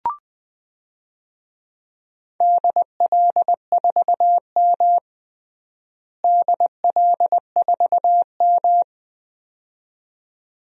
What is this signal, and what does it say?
Morse code, heard as a keyed tone, about 20 wpm, 700 Hz. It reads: DL4M DL4M
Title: DXCC-100 Call Signs - 4 Characters - Repeated 2x - 20wpm